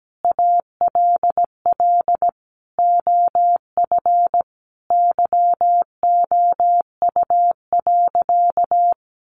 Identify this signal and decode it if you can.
Morse code, heard as a keyed tone, about 17 wpm, 700 Hz. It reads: ALL OF YOU.
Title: Sentences from Top 500 Words 17wpm